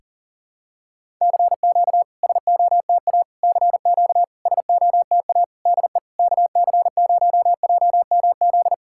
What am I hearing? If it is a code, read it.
Morse code, heard as a keyed tone, about 43 wpm, 700 Hz. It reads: CQ SOTA CQ SOTA DE KC0JMZ
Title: SOTA QSO Practice 40wpm